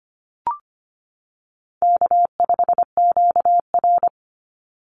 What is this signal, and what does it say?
Morse code, heard as a keyed tone, about 25 wpm, 700 Hz. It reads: K5QR
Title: NA SOTA Chaser Callsigns 25wpm